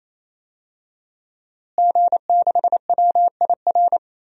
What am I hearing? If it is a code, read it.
Morse code, heard as a keyed tone, about 27 wpm, 700 Hz. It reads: G6WIR
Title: DXCC-100 Call Signs - 5 Characters - Repeated 3x - 28wpm